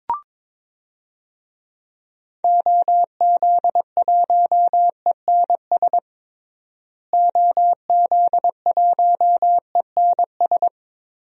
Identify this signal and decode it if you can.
Morse code, heard as a keyed tone, about 22 wpm, 700 Hz. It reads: OZ1ENS OZ1ENS
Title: DXCC-100 Call Signs - 6 Characters - Repeated 2x - 22wpm